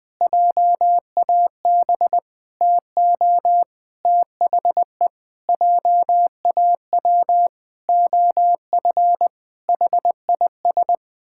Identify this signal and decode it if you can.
Morse code, heard as a keyed tone, about 20 wpm, 700 Hz. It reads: JAB TO THE JAW OF HIS
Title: Pangram Challenge v2 - 20wpm